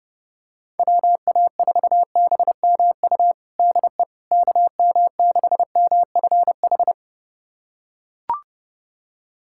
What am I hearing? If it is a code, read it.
Morse code, heard as a keyed tone, about 30 wpm, 700 Hz. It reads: WA4BMU DE KM6MFH
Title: USA Callsign DE Callsign 30wpm